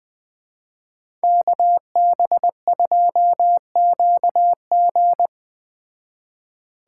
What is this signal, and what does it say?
Morse code, heard as a keyed tone, about 20 wpm, 700 Hz. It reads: KB2QG
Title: Speed Racing - USA 2x2 Call Signs 17wpm